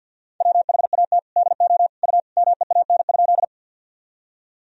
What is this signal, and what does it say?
Morse code, heard as a keyed tone, about 52 wpm, 700 Hz. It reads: WHAT DO U MEAN?
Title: Sentences from Top 200 Words and Abbreviations 50wpm